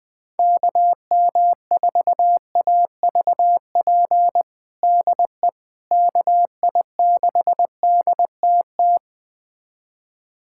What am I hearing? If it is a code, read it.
Morse code, heard as a keyed tone, about 21 wpm, 700 Hz. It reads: KM4AVP DE KI6DTT
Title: USA Callsign DE Callsign - 6 Characters - 20wpm